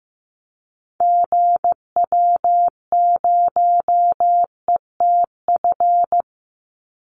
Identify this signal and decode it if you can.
Morse code, heard as a keyed tone, about 15 wpm, 700 Hz. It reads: GW0ETF
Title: Call Signs Repeated 3x - 15wpm